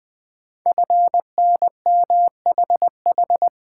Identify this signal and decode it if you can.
Morse code, heard as a keyed tone, about 20 wpm, 700 Hz. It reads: FNMHH
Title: Code Groups - 5 Letters - 20wpm